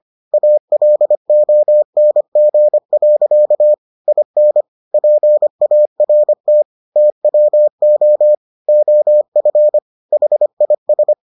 Morse code, heard as a keyed tone, about 25 wpm, 600 Hz. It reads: ALONG. IN PART TWO OF HIS